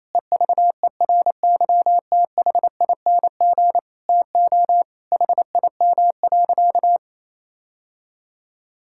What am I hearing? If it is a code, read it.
Morse code, heard as a keyed tone, about 28 wpm, 700 Hz. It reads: EVERYTHING TO HIM.
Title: Sentences from Top 1600 Words 28wpm